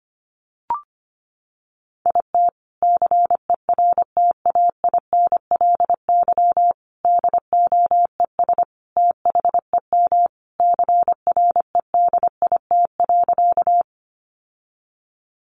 Morse code, heard as a keyed tone, about 25 wpm, 700 Hz. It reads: IT CERTAINLY DOES THEM CREDIT.
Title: Sentences from Top 1000 Words 25wpm